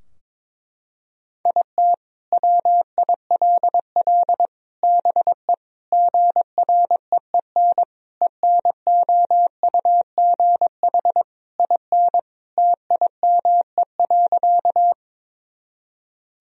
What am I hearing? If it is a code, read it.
Morse code, heard as a keyed tone, about 22 wpm, 700 Hz. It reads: IT WILL BE GREEN ENOUGH IN TIME.
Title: Sentences from Top 1000 Words - Review Part 2 - 22wpm